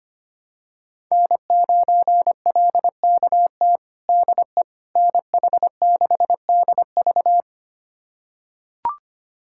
Morse code, heard as a keyed tone, about 25 wpm, 700 Hz. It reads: N9LKT DE NH6DV
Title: USA Callsign DE Callsign - 5 Characters - 25wpm